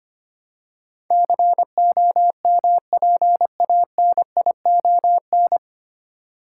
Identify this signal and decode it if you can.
Morse code, heard as a keyed tone, about 25 wpm, 700 Hz. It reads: COMPANION
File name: Nine Letter Words - 25wpm